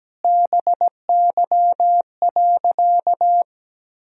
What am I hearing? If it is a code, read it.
Morse code, heard as a keyed tone, about 17 wpm, 700 Hz. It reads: BY.
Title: Sentences from Top 1000 Words - Review Part 1 - 17wpm